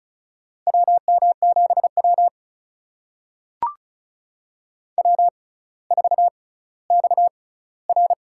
Morse code, heard as a keyed tone, about 36 wpm, spaced as for 18 wpm, 700 Hz. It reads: WM7W W4XR
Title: USA Call Signs - 4 Characters - 6x Character Spacing - 35wpm